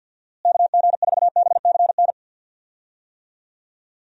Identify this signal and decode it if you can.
Morse code, heard as a keyed tone, about 48 wpm, 700 Hz. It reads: KG4BCN
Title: USA Call Signs - 6 Characters - Repeated 3x - 50wpm